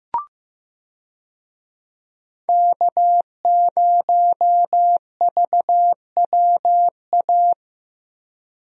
Morse code, heard as a keyed tone, about 15 wpm, 700 Hz. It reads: K0VWA
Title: USA Call Signs - 5 Characters - Repeated 3x - 15wpm